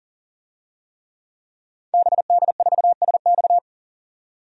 Morse code, heard as a keyed tone, about 40 wpm, 700 Hz. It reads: BD4SX
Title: DXCC-100 Call Signs - 5 Characters - 40wpm